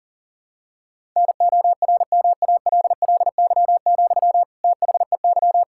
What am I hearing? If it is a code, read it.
Morse code, heard as a keyed tone, about 40 wpm, 700 Hz. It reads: NORMALLY, THEY
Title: Sentences from Top 1000 Words - Review Part 6 - 40wpm